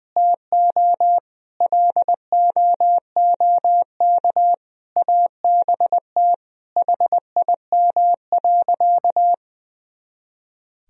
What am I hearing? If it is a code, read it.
Morse code, heard as a keyed tone, about 20 wpm, 700 Hz. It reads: TO LOOK ABT HIM.
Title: Sentences from Top 200 Words and Abbreviations 20wpm